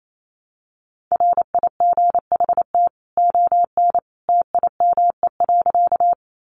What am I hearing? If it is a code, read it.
Morse code, heard as a keyed tone, about 28 wpm, 700 Hz. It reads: RIGHT ON TIME.
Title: A Sound Of Thunder 28wpm